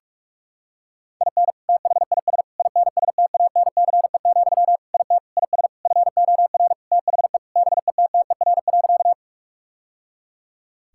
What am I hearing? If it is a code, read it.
Morse code, heard as a keyed tone, about 45 wpm, 690 Hz. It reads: IN THIS INSTANCE, IT IS FOR THE BETTER.
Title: Sentences from Top 1000 Words - Review Part 5 - 45wpm